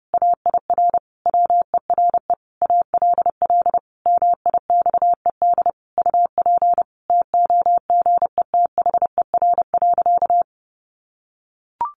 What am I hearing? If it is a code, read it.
Morse code, heard as a keyed tone, about 30 wpm, 700 Hz. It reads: AIR WERE ALL MIXED UP TOGETHER.
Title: Sentences from Top 1500 Words 30wpm